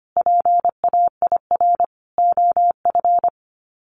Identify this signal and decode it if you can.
Morse code, heard as a keyed tone, about 25 wpm, 700 Hz. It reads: PAIR OF
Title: Sets of 3 Words 25wpm